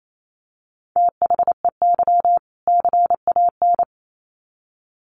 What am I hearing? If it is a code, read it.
Morse code, heard as a keyed tone, about 28 wpm, 700 Hz. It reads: THEY CAN
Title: Sets of 2 Words 28wpm